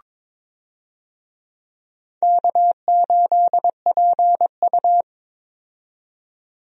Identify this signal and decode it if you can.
Morse code, heard as a keyed tone, about 22 wpm, 700 Hz. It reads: K8PU
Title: USA Call Signs - 4 Characters - 22wpm